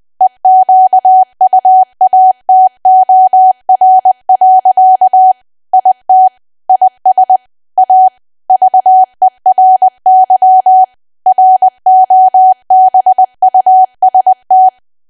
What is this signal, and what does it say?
Morse code, heard as a keyed tone, about 20 wpm, 750 Hz. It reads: EQUATOR. IT IS A VERY ROBUST
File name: cwa / W1AW-Level3 / 20WPM